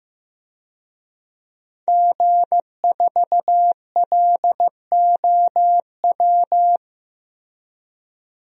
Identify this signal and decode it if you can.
Morse code, heard as a keyed tone, about 15 wpm, 700 Hz. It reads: G4LOW
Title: DXCC-100 Call Signs - 5 Characters - Repeated 2x - 15wpm